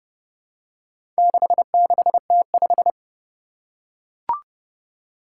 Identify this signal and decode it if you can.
Morse code, heard as a keyed tone, about 32 wpm, 700 Hz. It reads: 66T5
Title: Four Challenging Letter-Number Combinations 30wpm